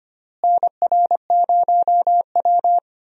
Morse code, heard as a keyed tone, about 25 wpm, 700 Hz. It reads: NR0W
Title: USA Call Signs - 4 Characters - Repeated 2x - 25wpm